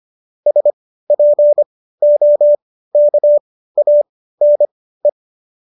Morse code, heard as a keyed tone, about 25 wpm, spaced as for 15 wpm, 590 Hz. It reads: SPOKANE